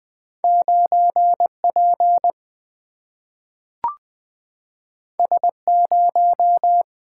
Morse code, heard as a keyed tone, about 20 wpm, 700 Hz. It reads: 9P S0
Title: Single-Letter Single-Number 20wpm